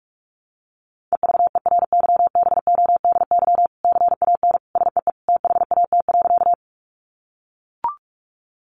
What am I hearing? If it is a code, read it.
Morse code, heard as a keyed tone, about 44 wpm, 700 Hz. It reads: EVERYBODY CAN SEE THAT.
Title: Sentences from Top 1400 Words 45wpm